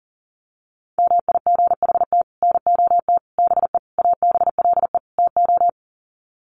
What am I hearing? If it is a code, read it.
Morse code, heard as a keyed tone, about 40 wpm, 700 Hz. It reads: MIGHT NOT BE ABLE TO
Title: Sets of 5 Words Encore2 40wpm